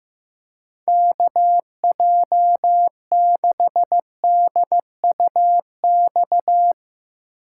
Morse code, heard as a keyed tone, about 15 wpm, 700 Hz. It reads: KJ6DUX
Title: USA Call Signs - 6 Characters - 8x Character Spacing - 15wpm